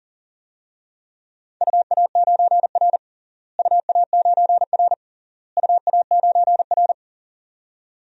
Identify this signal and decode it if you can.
Morse code, heard as a keyed tone, about 40 wpm, 700 Hz. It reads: UA9R UA9R UA9R
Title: DXCC-100 Call Signs - 4 Characters - Repeated 3x - 40wpm